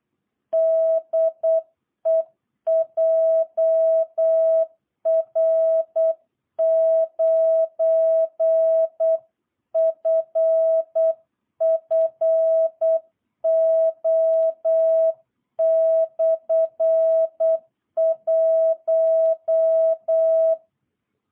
Morse code, heard as a keyed tone, about 8 wpm, 660 Hz. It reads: DE JR9FFO/1
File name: jh8cbh / 2022cw